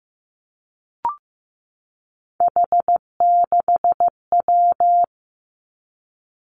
Morse code, heard as a keyed tone, about 15 wpm, 700 Hz. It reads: H6W
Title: Three Challenging Letter-Number Combinations 15wpm